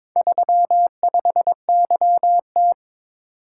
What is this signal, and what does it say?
Morse code, heard as a keyed tone, about 22 wpm, 700 Hz. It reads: 35YT
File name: practice / 22wpm / Four Letter-Number Combinations - 22wpm